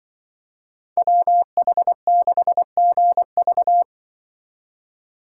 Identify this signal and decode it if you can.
Morse code, heard as a keyed tone, about 24 wpm, 700 Hz. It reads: WH6GV
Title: Speed Racing - USA 2x2 Call Signs 20wpm